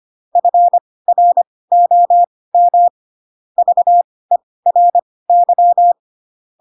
Morse code, heard as a keyed tone, about 25 wpm, spaced as for 18 wpm, 700 Hz. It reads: FROM VERY